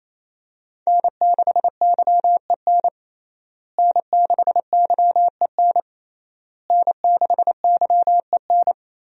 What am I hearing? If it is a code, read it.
Morse code, heard as a keyed tone, about 28 wpm, 700 Hz. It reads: N6YEN N6YEN N6YEN
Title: USA Call Signs - 5 Characters - Repeated 3x - 28wpm